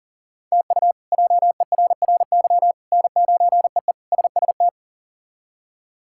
Morse code, heard as a keyed tone, about 37 wpm, 700 Hz. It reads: TU JERRY N9EE SST
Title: K1USN SST QSO Practice - 40wpm